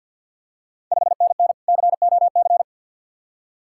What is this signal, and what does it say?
Morse code, heard as a keyed tone, about 52 wpm, 690 Hz. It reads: 5NN COC
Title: Ontario QSO Party Exchange 50wpm